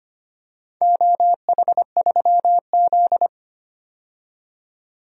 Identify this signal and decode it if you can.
Morse code, heard as a keyed tone, about 25 wpm, 700 Hz. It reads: OH3Z
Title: DXCC-100 Call Signs - 4 Characters - 25wpm